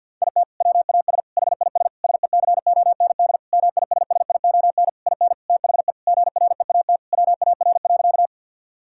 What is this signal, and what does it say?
Morse code, heard as a keyed tone, about 52 wpm, 700 Hz. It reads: IT WAS HIS SECOND MISSION IN THE GREAT WAR.
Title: Sentences from Top 1600 Words 50wpm